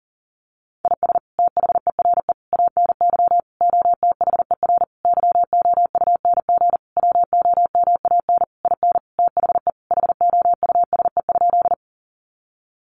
Morse code, heard as a keyed tone, about 40 wpm, 700 Hz. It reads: IS THERE ANY OTHER YOUNG WOMAN IN THE HOUSE?
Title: Sentences from Top 1100 Words 40wpm